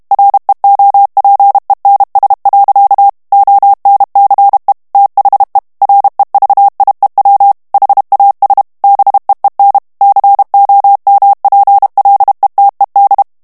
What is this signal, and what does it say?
Morse code, heard as a keyed tone, about 32 wpm, 800 Hz. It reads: REOPENS. ONCE THE REVIEW HAS BEEN COMPLETED
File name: ANGLAIS texte clair 3 vitesse 30 mots minute